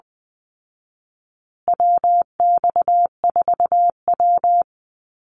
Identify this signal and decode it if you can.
Morse code, heard as a keyed tone, about 20 wpm, 700 Hz. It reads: WX4W
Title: Call Signs Repeated 3x - 20wpm